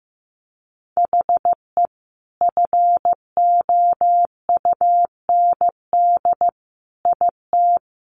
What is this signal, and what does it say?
Morse code, heard as a keyed tone, about 15 wpm, 700 Hz. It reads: HE FOUND IT
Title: Sentences from Top 1400 Words 15wpm